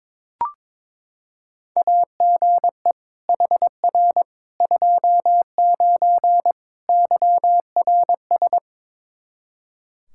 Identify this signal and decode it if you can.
Morse code, heard as a keyed tone, about 22 wpm, 700 Hz. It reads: AGE HR 29 YRS
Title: Practice QSO - Exchange 3 - More Info - 22wpm